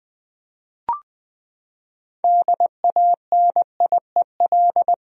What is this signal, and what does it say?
Morse code, heard as a keyed tone, about 20 wpm, 700 Hz. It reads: DANIEL
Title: USA Common Names - 20wpm